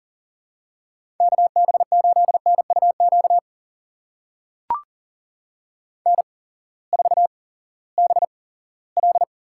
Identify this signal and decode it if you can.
Morse code, heard as a keyed tone, about 41 wpm, spaced as for 20 wpm, 700 Hz. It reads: KB8NUQ N4BL
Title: USA Call Signs - 8x Character Spacing - 40wpm